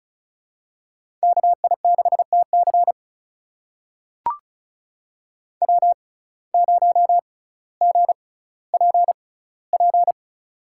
Morse code, heard as a keyed tone, about 34 wpm, spaced as for 18 wpm, 700 Hz. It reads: KI6TC W0GPP
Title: USA Call Signs - 5 Characters - 6x Character Spacing - 35wpm